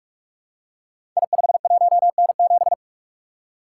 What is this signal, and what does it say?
Morse code, heard as a keyed tone, about 44 wpm, 700 Hz. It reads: I51N7
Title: Code Groups - 5 Letter:Numbers - 45wpm